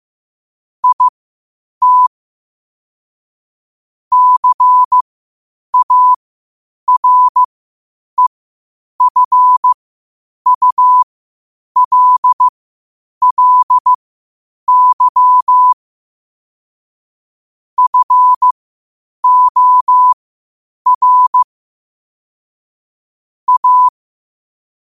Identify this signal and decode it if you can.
Morse code, heard as a keyed tone, about 15 wpm, spaced as for 8 wpm, 1000 Hz. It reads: IT CAREFULLY FOR A
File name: qotdfiles / 8wpm